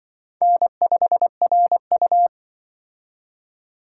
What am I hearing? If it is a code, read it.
Morse code, heard as a keyed tone, about 24 wpm, 700 Hz. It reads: N5RU
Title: Speed Racing - USA 1x2 Call Signs 20wpm